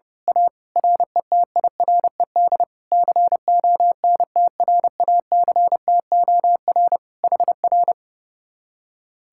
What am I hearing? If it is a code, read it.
Morse code, heard as a keyed tone, about 28 wpm, 700 Hz. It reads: A RETIRED CONTRACTOR HR
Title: Speed-Racing Common QSO Phrases Encore - 30wpm